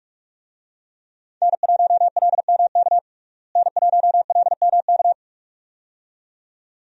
Morse code, heard as a keyed tone, about 47 wpm, 700 Hz. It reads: N1LMK N1LMK
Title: USA Call Signs - 5 Characters - Repeated 2x - 45wpm